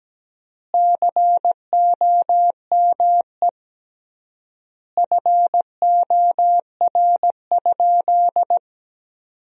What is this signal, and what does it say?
Morse code, heard as a keyed tone, about 17 wpm, 700 Hz. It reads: COME FOR?
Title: 3x Word-Spacing - Sentences from Top 100 Words - 17wpm